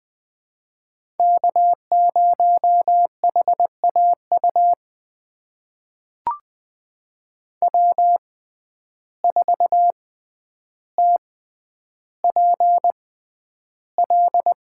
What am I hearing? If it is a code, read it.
Morse code, heard as a keyed tone, about 20 wpm, spaced as for 10 wpm, 700 Hz. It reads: K0HAU W4TPL
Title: USA Call Signs - 5 Characters - 6x Character Spacing - 20wpm